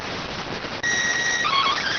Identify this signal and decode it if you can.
SSTV (slow-scan television), PD 290.